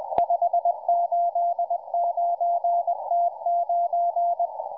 Morse code, heard as a keyed tone, about 20 wpm, 690 Hz, with radio noise. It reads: 589T9